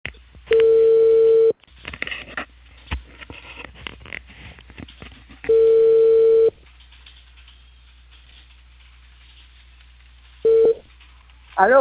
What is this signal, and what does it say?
Ringback tone (China, 450 Hz), with speech.